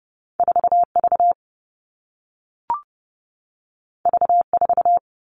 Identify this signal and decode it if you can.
Morse code, heard as a keyed tone, about 30 wpm, 700 Hz. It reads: 4V V4
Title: All Cut Numbers - 2-Digit 30wpm